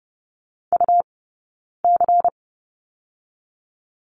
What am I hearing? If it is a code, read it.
Morse code, heard as a keyed tone, about 30 wpm, 700 Hz. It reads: U C
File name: course / 30:10wpm / 099 Lesson 17d - Letter u - Two Characters 30wpm